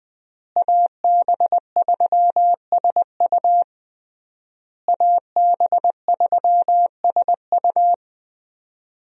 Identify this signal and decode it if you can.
Morse code, heard as a keyed tone, about 21 wpm, 700 Hz. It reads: AB3SU AB3SU
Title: USA Call Signs - 5 Characters - Repeated 3x - 20wpm